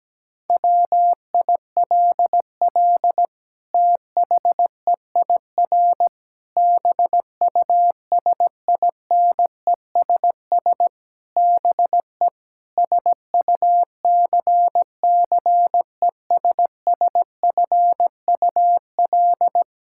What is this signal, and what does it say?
Morse code, heard as a keyed tone, about 17 wpm, 700 Hz. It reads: WILL THEIR BUSINESS BE SUCCESSFUL
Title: Sentences from Top 1000 Words - Review Part 5 - 17wpm